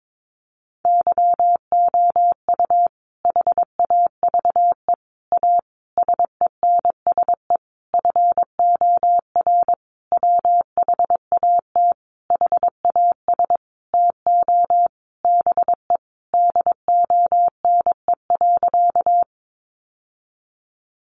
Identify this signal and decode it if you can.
Morse code, heard as a keyed tone, about 22 wpm, 700 Hz. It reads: YOU HAVE A SENSE FOR WHAT HAS TO BE DONE.
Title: Sentences from Top 700 Words 22wpm